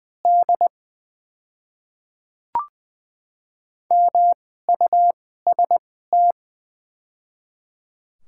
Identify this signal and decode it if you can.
Morse code, heard as a keyed tone, about 21 wpm, spaced as for 15 wpm, 700 Hz. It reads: D MUST
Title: 2x Character Spacing - Top 500 Words - 20wpm